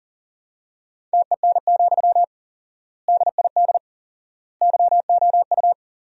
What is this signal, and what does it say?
Morse code, heard as a keyed tone, about 40 wpm, 700 Hz. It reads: TEN, DID YOU
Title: 4x Word-Spacing - Sentences from Top 400 Words - 40wpm